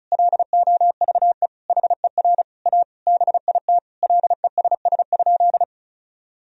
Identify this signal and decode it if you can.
Morse code, heard as a keyed tone, about 34 wpm, 700 Hz. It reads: LOVE HER A BIT LESS?
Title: Sentences from Top 1000 Words - Review Part 2 - 35wpm